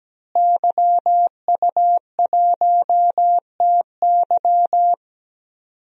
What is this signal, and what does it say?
Morse code, heard as a keyed tone, about 17 wpm, 700 Hz. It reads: YU1TY
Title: DXCC-100 Call Signs - 5 Characters - Repeated 2x - 17wpm